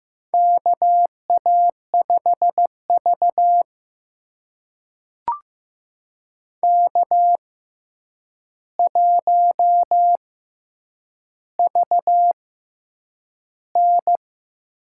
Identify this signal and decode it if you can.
Morse code, heard as a keyed tone, about 15 wpm, spaced as for 8 wpm, 700 Hz. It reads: KA5V K1VN
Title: USA Call Signs - 4 Characters - 6x Character Spacing - 15wpm